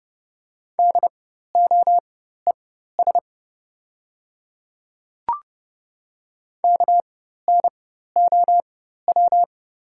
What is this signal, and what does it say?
Morse code, heard as a keyed tone, about 30 wpm, spaced as for 14 wpm, 700 Hz. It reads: DOES KNOW
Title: 4x Character Spacing - Top 200 Words - 30wpm